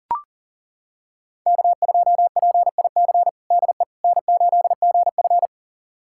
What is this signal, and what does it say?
Morse code, heard as a keyed tone, about 40 wpm, 700 Hz. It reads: K2PIC DE N8GF
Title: USA Callsign DE Callsign 40wpm